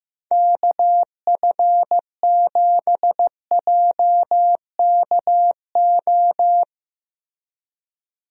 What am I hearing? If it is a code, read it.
Morse code, heard as a keyed tone, about 15 wpm, 700 Hz. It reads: KF7JKO
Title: USA Call Signs - 6 Characters - Repeated 2x - 15wpm